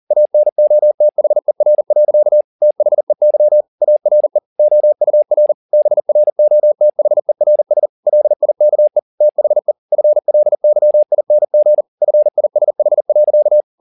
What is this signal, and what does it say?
Morse code, heard as a keyed tone, about 40 wpm, 600 Hz. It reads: ANOTHER. THEY ARE OUR BROTHERS LIKE THE FLYING FISH.